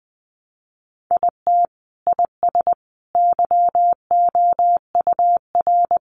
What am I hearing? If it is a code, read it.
Morse code, heard as a keyed tone, about 20 wpm, 700 Hz. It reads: IT IS YOUR
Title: Sentences from Top 1000 Words 20wpm